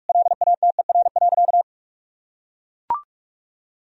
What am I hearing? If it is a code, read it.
Morse code, heard as a keyed tone, about 42 wpm, 700 Hz. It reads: LATER.
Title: Sentences from Top 1000 Words - Review Part 3 - 45wpm